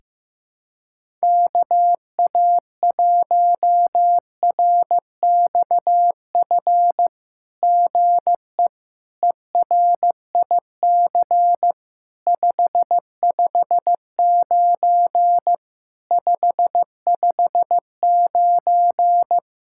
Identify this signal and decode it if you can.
Morse code, heard as a keyed tone, about 15 wpm, 700 Hz. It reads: KA1RXF GE ERIC 559 559
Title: SOTA QSO Practice 15wpm